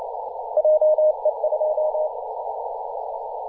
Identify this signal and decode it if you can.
Morse code, heard as a keyed tone, about 29 wpm, 620 Hz, with noise. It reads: JE2
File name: posts / 2017-aug-19-2106z-a35jp-p-14-005mhz-cw